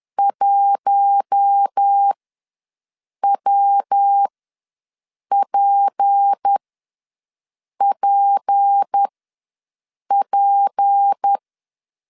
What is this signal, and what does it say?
Morse code, heard as a keyed tone, about 10 wpm, spaced as for 6 wpm, 800 Hz. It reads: 1WPPP
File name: LEZIONI / 6 PKX0 / 6 P ws 10wpm 3751